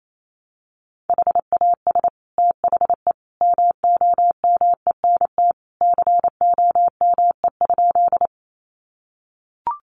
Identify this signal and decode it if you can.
Morse code, heard as a keyed tone, about 28 wpm, 700 Hz. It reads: HAS THE MOMENT COME?
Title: Sentences from Top 1000 Words 28wpm